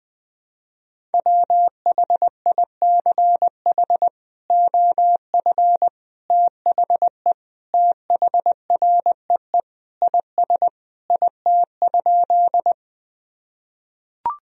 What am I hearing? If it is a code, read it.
Morse code, heard as a keyed tone, about 21 wpm, 700 Hz. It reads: WHICH OF THE THREE IS IT?
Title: Sentences from Top 1000 Words - Review Part 1 - 20wpm